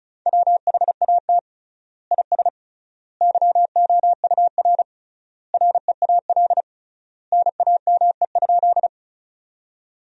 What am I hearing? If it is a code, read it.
Morse code, heard as a keyed tone, about 36 wpm, 700 Hz. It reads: WHAT IS YOUR REAL NAME?
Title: 3x Word-Spacing - Sentences from Top 300 Words - 35wpm